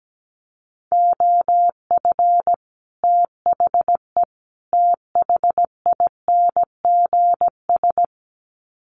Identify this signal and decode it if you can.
Morse code, heard as a keyed tone, about 17 wpm, 700 Hz. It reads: OF THE THINGS
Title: Sets of 3 Words 17wpm